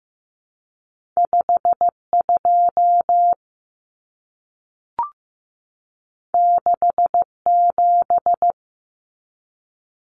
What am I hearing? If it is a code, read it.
Morse code, heard as a keyed tone, about 15 wpm, 700 Hz. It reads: 52 67
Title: Common Cut Numbers - 2-Digit 15wpm